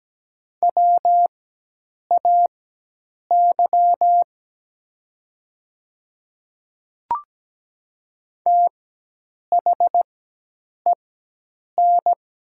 Morse code, heard as a keyed tone, about 17 wpm, spaced as for 8 wpm, 700 Hz. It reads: WAY THEN
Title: 4x Character Spacing - Top 100 Words - 17wpm